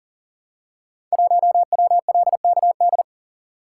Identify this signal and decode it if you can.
Morse code, heard as a keyed tone, about 42 wpm, 700 Hz. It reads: 1WLKD
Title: Code Groups - 5 Letter:Numbers - 40wpm